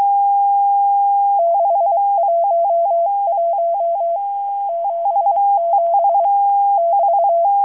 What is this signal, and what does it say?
Morse code, heard as a keyed tone, about 23 wpm, 680 Hz. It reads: 611 77 <BT>
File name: HM02- -0437utc-7351khz